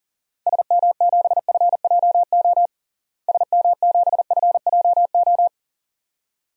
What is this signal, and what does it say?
Morse code, heard as a keyed tone, about 40 wpm, 700 Hz. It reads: SM7FJO SM7FJO
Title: DXCC-100 Call Signs - 6 Characters - Repeated 2x - 40wpm